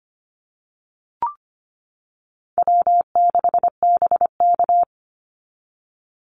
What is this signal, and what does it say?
Morse code, heard as a keyed tone, about 25 wpm, 700 Hz. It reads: W6BK
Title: Call Signs - 25wpm